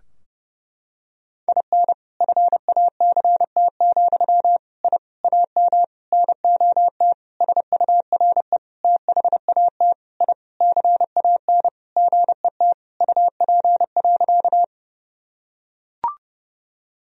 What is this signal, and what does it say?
Morse code, heard as a keyed tone, about 30 wpm, 700 Hz. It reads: IN FACT, I AM NOT SURE THAT I CAN GET UP.
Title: Sentences from Top 1000 Words - Review Part 3 - 30wpm